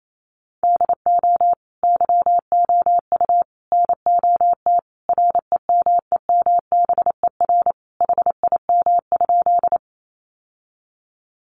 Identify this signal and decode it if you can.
Morse code, heard as a keyed tone, about 28 wpm, 700 Hz. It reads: DO YOU NOT REMEMBER HIM?
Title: Sentences from Top 600 Words 28wpm